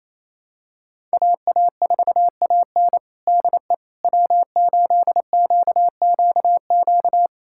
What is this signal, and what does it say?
Morse code, heard as a keyed tone, about 28 wpm, 700 Hz. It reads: AA4AN DE W8QQQ
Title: USA Callsign DE Callsign 28wpm